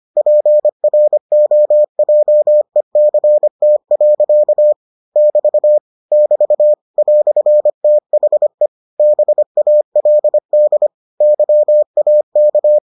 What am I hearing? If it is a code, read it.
Morse code, heard as a keyed tone, about 25 wpm, 600 Hz. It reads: PROJECT. <BT> <BT> "THE BALD YAK